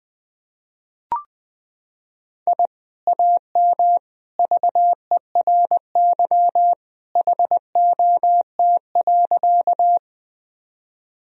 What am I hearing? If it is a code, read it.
Morse code, heard as a keyed tone, about 20 wpm, 700 Hz. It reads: I AM VERY HOT.
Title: Sentences from Top 1000 Words - Review Part 3 - 20wpm